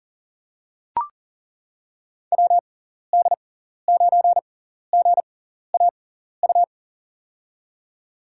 Morse code, heard as a keyed tone, about 40 wpm, spaced as for 14 wpm, 700 Hz. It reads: WD9GAU
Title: USA Call Signs - 6 Characters - 6x Character Spacing - 40wpm